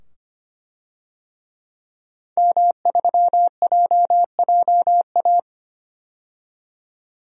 Morse code, heard as a keyed tone, about 25 wpm, 700 Hz. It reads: M3JJA
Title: DXCC-100 Call Signs - 5 Characters - 25wpm